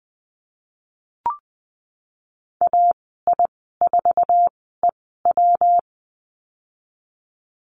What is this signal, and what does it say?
Morse code, heard as a keyed tone, about 20 wpm, spaced as for 15 wpm, 700 Hz. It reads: AI4EW
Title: 128 Lesson 21g - Number 6 - Call Signs 20wpm